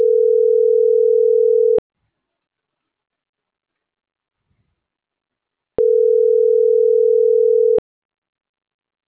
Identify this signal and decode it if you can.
Ringback tone (North America), 440 + 480 Hz.